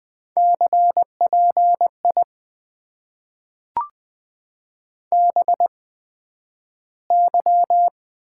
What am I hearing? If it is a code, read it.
Morse code, heard as a keyed tone, about 20 wpm, spaced as for 11 wpm, 700 Hz. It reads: CPI BY
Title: Top 90 QSO Elements - 8x Character Spacing - 20wpm